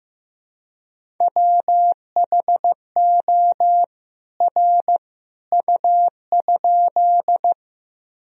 Morse code, heard as a keyed tone, about 15 wpm, 700 Hz. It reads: WHO R U?
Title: Sentences from Top 100 Words and Abbreviations 15wpm